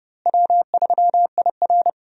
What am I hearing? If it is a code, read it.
Morse code, heard as a keyed tone, about 30 wpm, 700 Hz. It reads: W3IR
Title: USA Call Signs - 4 Characters - Repeated 3x - 30wpm